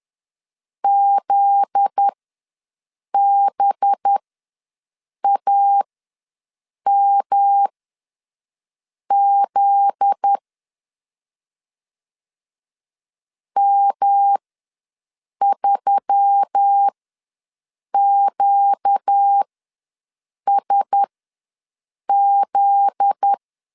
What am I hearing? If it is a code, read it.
Morse code, heard as a keyed tone, about 10 wpm, spaced as for 6 wpm, 800 Hz. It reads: ZBAMZ M3QSZ
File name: LEZIONI / 7 GF9Z / 7 GF9Z ws 10wpm 3919